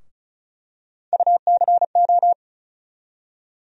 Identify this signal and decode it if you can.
Morse code, heard as a keyed tone, about 36 wpm, 700 Hz. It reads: UCO
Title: USA Call Sign Postfixes - 3 Letters 35wpm